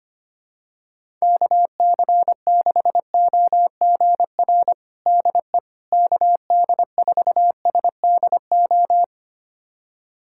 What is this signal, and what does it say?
Morse code, heard as a keyed tone, about 25 wpm, 700 Hz. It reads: KC6OGR DE KD4SDO
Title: USA Callsign DE Callsign 25wpm